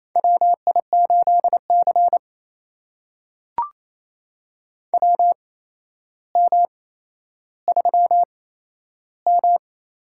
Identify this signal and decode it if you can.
Morse code, heard as a keyed tone, about 28 wpm, spaced as for 12 wpm, 700 Hz. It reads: WI8C WM3M
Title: USA Call Signs - 4 Characters - 8x Character Spacing - 28wpm